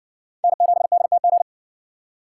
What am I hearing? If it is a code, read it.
Morse code, heard as a keyed tone, about 55 wpm, 700 Hz. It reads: N7DTG